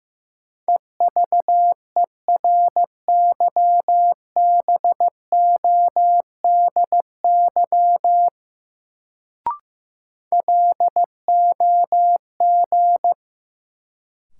Morse code, heard as a keyed tone, about 15 wpm, 700 Hz. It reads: EVERYBODY LOG
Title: Sentences from Top 1400 Words - New Word Drill - 15wpm